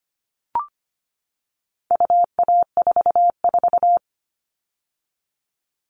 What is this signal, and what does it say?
Morse code, heard as a keyed tone, about 25 wpm, 700 Hz. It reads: UA44
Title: All Cut Numbers - 4-Digit 25wpm